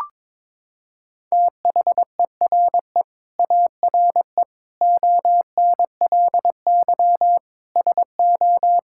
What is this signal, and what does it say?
Morse code, heard as a keyed tone, about 22 wpm, 700 Hz. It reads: THERE ARE ONLY SO
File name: Sets of 9 Words - 22wpm